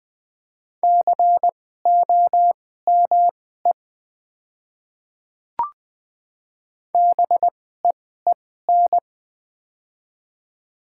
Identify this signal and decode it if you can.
Morse code, heard as a keyed tone, about 20 wpm, spaced as for 14 wpm, 700 Hz. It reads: COME BEEN
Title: 2x Character Spacing - Top 100 Words - 20wpm